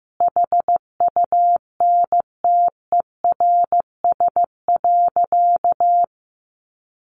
Morse code, heard as a keyed tone, about 15 wpm, 700 Hz. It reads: HUNTERS.
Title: A Sound Of Thunder 15wpm